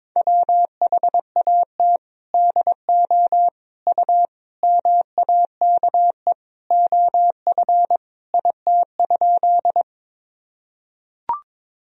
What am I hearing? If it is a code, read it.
Morse code, heard as a keyed tone, about 22 wpm, 700 Hz. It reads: WHAT DO U MAKE OF IT?
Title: Sentences from Top 100 Words and Abbreviations 22wpm